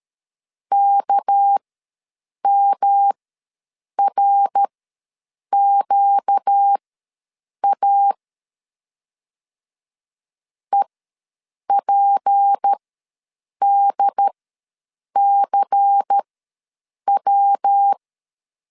Morse code, heard as a keyed tone, about 13 wpm, spaced as for 7 wpm, 800 Hz. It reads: KMRQA EPDCW
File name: LEZIONI / 6 PKX0 / 6 PKX ws 12wpm 3106